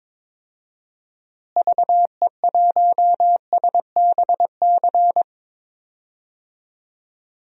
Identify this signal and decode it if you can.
Morse code, heard as a keyed tone, about 21 wpm, 700 Hz. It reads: VE1SBC
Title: DXCC-100 Call Signs - 6 Characters - Repeated 2x - 22wpm